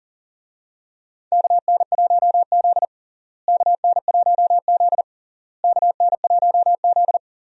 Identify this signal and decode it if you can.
Morse code, heard as a keyed tone, about 40 wpm, 700 Hz. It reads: KN1Z KN1Z KN1Z
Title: USA Call Signs - 4 Characters - Repeated 3x - 40wpm